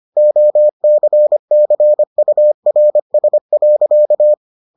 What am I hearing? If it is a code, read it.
Morse code, heard as a keyed tone, about 25 wpm, 600 Hz. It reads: OCCURS.